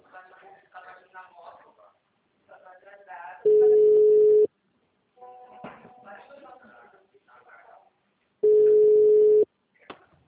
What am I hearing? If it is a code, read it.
Ringback tone (Europe, 425 Hz), with speech.